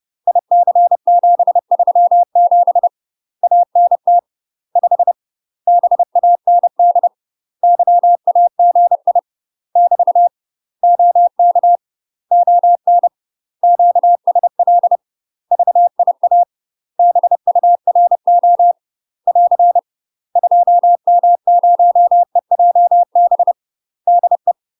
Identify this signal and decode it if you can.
Morse code, heard as a keyed tone, about 30 wpm, 700 Hz. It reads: IC737 ANT 5 BAND YAGI <BT> OK ON QSL VIA BURO <AR> 2M0EJB DE
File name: UK MASTER QSOs TRAINING / 30WPM / UK MASTER QSOs 0038